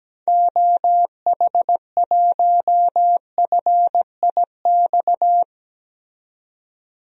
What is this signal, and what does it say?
Morse code, heard as a keyed tone, about 17 wpm, 700 Hz. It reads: OH1FIX
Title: DXCC-100 Call Signs - 6 Characters - 17wpm